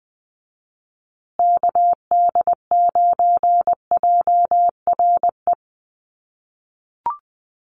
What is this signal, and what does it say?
Morse code, heard as a keyed tone, about 20 wpm, 700 Hz. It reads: KD9JRE
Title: USA Call Signs Repeated 2x 20wpm